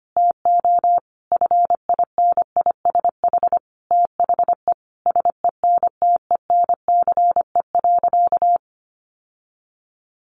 Morse code, heard as a keyed tone, about 25 wpm, 700 Hz. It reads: TO FINISH THE SENTENCE.
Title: Sentences from Top 1200 Words 25wpm